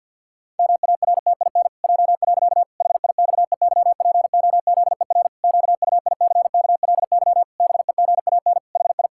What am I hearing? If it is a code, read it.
Morse code, heard as a keyed tone, about 50 wpm, 700 Hz. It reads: MARTIN J. HIXEYPOZER QUICKLY BEGAN HI
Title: Pangram Challenge v2 - 50wpm